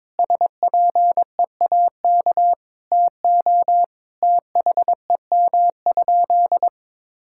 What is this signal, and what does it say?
Morse code, heard as a keyed tone, about 22 wpm, 700 Hz. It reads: SPEAK TO THEM?
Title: Sentences from Top 1600 Words 22wpm